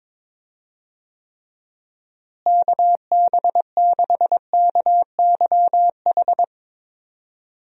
Morse code, heard as a keyed tone, about 22 wpm, 700 Hz. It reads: KB6KYH
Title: USA Call Signs - 6 Characters - Repeated 2x - 22wpm